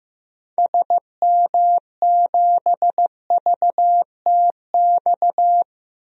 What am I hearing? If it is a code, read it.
Morse code, heard as a keyed tone, about 15 wpm, 700 Hz. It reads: SM7VTX
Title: DXCC-100 Call Signs - 6 Characters - 15wpm